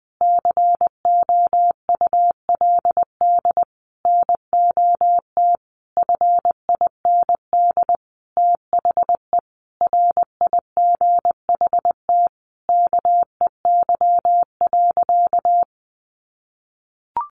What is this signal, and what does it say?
Morse code, heard as a keyed tone, about 20 wpm, 700 Hz. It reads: COULD NOT FIND THE RIGHT KEY.
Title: Sentences from Top 600 Words 20wpm